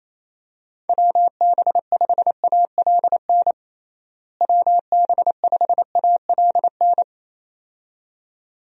Morse code, heard as a keyed tone, about 28 wpm, 700 Hz. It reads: WB5ALN WB5ALN
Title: USA Call Signs - 6 Characters - Repeated 2x - 28wpm